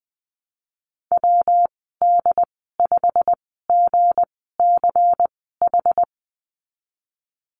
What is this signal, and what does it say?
Morse code, heard as a keyed tone, about 20 wpm, spaced as for 15 wpm, 700 Hz. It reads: WD5GCH
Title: 193 Lesson 31g - Letter v - Call Signs 20wpm